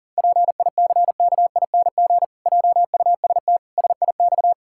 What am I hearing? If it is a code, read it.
Morse code, heard as a keyed tone, about 40 wpm, 700 Hz. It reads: PICKING JUST SIX
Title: Pangram Challenge v2 - 40wpm